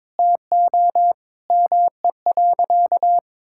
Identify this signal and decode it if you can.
Morse code, heard as a keyed tone, about 22 wpm, 700 Hz. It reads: TO ME.
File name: practice / 22wpm / Sentences with 8-Word Phrases - 22wpm